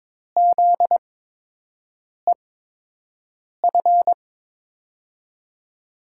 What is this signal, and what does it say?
Morse code, heard as a keyed tone, about 22 wpm, spaced as for 6 wpm, 700 Hz. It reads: ZEF
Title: USA Call Signs - 6 Characters - 8x Character Spacing - 22wpm